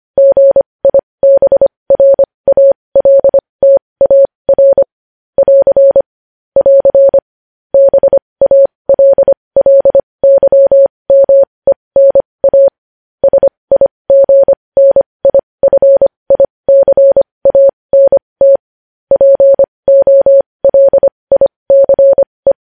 Morse code, heard as a keyed tone, about 25 wpm, spaced as for 20 wpm, 550 Hz. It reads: GIBRALTAR <AR> <AR> BALLYMENA SIGNIFICANT POLICE
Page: News Headlines in Morse Code at 20 WPM